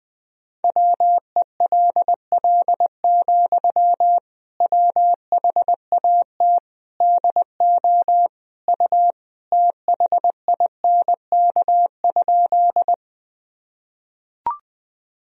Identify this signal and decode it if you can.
Morse code, heard as a keyed tone, about 20 wpm, 700 Hz. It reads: WELL, WHAT DO U THINK?
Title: Sentences from Top 200 Words and Abbreviations 20wpm